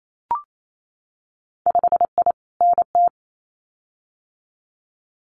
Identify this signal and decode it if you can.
Morse code, heard as a keyed tone, about 28 wpm, 700 Hz. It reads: 5I NT
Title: Winter Field Day Exchanges Encore 28wpm